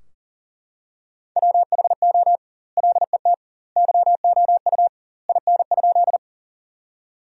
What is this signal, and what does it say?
Morse code, heard as a keyed tone, about 40 wpm, 700 Hz. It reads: WHO LET YOU IN?
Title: 2x Word-Spacing - Sentences from Top 300 Words - 40wpm